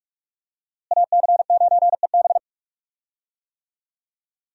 Morse code, heard as a keyed tone, about 47 wpm, 700 Hz. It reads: AC9EB
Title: USA Call Signs - 5 Characters - Repeated 3x - 45wpm